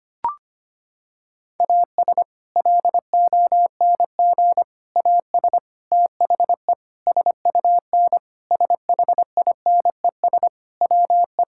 Morse code, heard as a keyed tone, about 25 wpm, 700 Hz. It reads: AS LONG AS THE SUN SHINES WE
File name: practice / 25wpm / Sentences with 4-Word Phrases - 25wpm